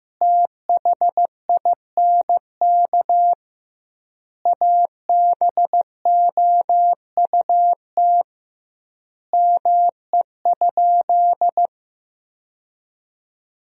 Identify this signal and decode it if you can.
Morse code, heard as a keyed tone, about 14 wpm, 700 Hz. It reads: THINK ABOUT ME?
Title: 2x Word-Spacing - Sentences from Top 200 Words - 15wpm